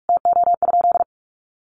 Morse code, heard as a keyed tone, about 44 wpm, 700 Hz. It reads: TO?
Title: Sentences from Top 1300 Words 45wpm